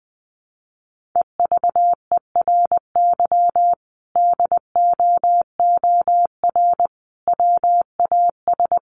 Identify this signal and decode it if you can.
Morse code, heard as a keyed tone, about 20 wpm, 700 Hz. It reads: EVERY DOOR WAS
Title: Sentences from Top 1400 Words 20wpm